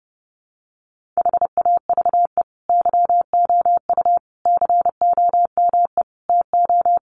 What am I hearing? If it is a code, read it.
Morse code, heard as a keyed tone, about 30 wpm, 700 Hz. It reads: HAVE YOU COME TO
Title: Sentences from Top 1400 Words 30wpm